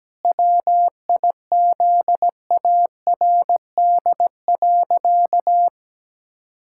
Morse code, heard as a keyed tone, about 17 wpm, 700 Hz. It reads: WIZARD.
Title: Pangram Challenge v2 - 17wpm